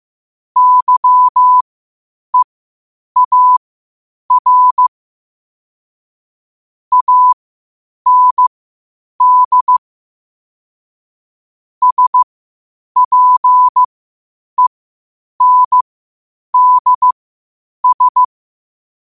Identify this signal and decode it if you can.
Morse code, heard as a keyed tone, about 15 wpm, spaced as for 8 wpm, 1000 Hz. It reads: YEAR AND SPENDS
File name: qotdfiles / 8wpm